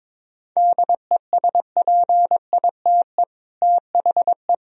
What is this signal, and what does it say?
Morse code, heard as a keyed tone, about 22 wpm, 700 Hz. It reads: DESPITE THE
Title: Speed-Racing - Sets of 4 Words 22wpm